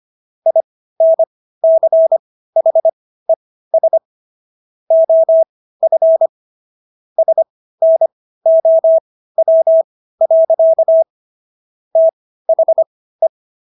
Morse code, heard as a keyed tone, about 25 wpm, spaced as for 15 wpm, 650 Hz. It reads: INCHES OF SNOW. THE